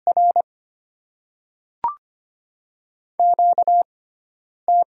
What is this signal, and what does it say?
Morse code, heard as a keyed tone, about 25 wpm, spaced as for 9 wpm, 700 Hz. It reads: R QT
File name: Top 20 QSO Elements - 6x Character Spacing - 25wpm